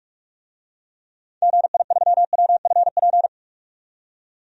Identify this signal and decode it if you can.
Morse code, heard as a keyed tone, about 46 wpm, 700 Hz. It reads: GI3WFP
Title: DXCC-100 Call Signs - 6 Characters - 45wpm